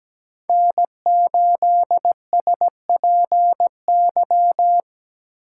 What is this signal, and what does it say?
Morse code, heard as a keyed tone, about 17 wpm, 700 Hz. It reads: N8SPY
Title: USA Call Signs - 5 Characters - 3x Character Spacing - 17wpm